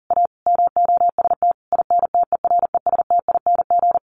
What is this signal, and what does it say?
Morse code, heard as a keyed tone, about 40 wpm, 700 Hz. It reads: A MOST INTERESTING
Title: Sentences from Top 1000 Words 40wpm